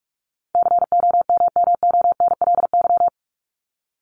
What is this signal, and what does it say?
Morse code, heard as a keyed tone, about 44 wpm, 700 Hz. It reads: COMMONLY
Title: LY Suffix - Top 100 Words with Base Words 45wpm